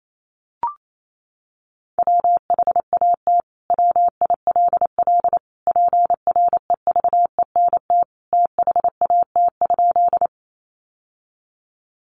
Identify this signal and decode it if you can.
Morse code, heard as a keyed tone, about 28 wpm, 700 Hz. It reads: WHAT WILL PREVENT THAT?
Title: Sentences from Top 1100 Words 28wpm